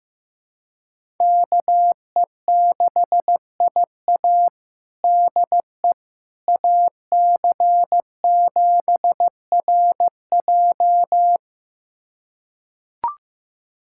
Morse code, heard as a keyed tone, about 15 wpm, 700 Hz. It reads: KE6IA DE AC7RJ
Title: USA Callsign DE Callsign - 5 Characters - 15wpm